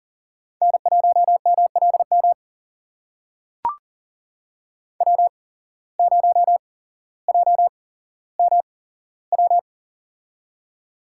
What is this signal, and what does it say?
Morse code, heard as a keyed tone, about 40 wpm, spaced as for 17 wpm, 700 Hz. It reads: N1MLM W0JMW
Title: USA Call Signs - 5 Characters - 8x Character Spacing - 40wpm